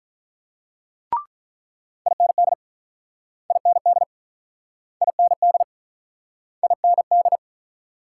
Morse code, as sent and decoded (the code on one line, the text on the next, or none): .. -. -.. / .. -. -.. / .. -. -.. / .. -. -..
IND IND IND IND